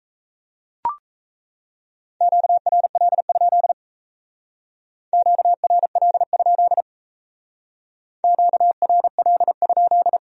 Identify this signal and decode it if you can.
Morse code, heard as a keyed tone, about 36 wpm, 700 Hz. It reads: QRL? QRL? QRL?